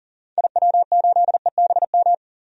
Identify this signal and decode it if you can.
Morse code, heard as a keyed tone, about 40 wpm, 700 Hz. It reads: IW8EBM
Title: DXCC-100 Call Signs - 6 Characters - 40wpm